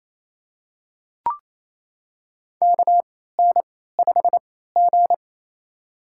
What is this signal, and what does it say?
Morse code, heard as a keyed tone, about 28 wpm, spaced as for 16 wpm, 700 Hz. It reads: KN5G
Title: USA Call Signs - 4 Characters - 3x Character Spacing - 28wpm